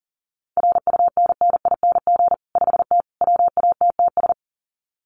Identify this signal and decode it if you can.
Morse code, heard as a keyed tone, about 40 wpm, 700 Hz. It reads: RUNNING 5T WATTS
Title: Common QSO Phrases 40wpm